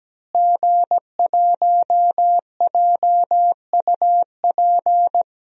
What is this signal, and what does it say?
Morse code, heard as a keyed tone, about 17 wpm, 700 Hz. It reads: G1JUP
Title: DXCC-100 Call Signs - 5 Characters - Repeated 3x - 17wpm